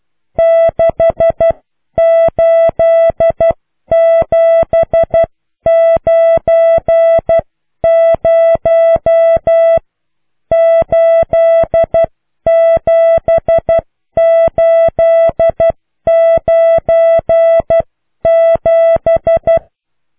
Morse code, heard as a keyed tone, about 12 wpm, spaced as for 10 wpm, 660 Hz. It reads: 6879087897